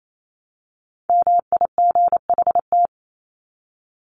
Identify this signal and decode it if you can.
Morse code, heard as a keyed tone, about 28 wpm, 700 Hz. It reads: MIGHT
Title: Top 300 Words 28wpm